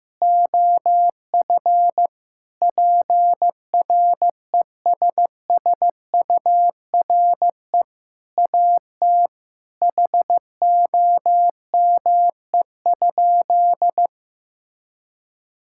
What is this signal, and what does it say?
Morse code, heard as a keyed tone, about 15 wpm, 700 Hz. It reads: OF PRESSURE AT HOME?
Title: Sentences from Top 1000 Words - Review Part 6 - 15wpm